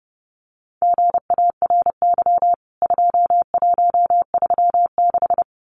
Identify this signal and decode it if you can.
Morse code, heard as a keyed tone, about 30 wpm, 700 Hz. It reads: GARY 2136
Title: CWT Exchange 30wpm